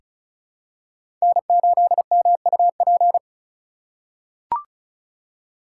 Morse code, heard as a keyed tone, about 36 wpm, 700 Hz. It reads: N8MUP
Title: USA Call Signs - 5 Characters - 2x Character Spacing - 35wpm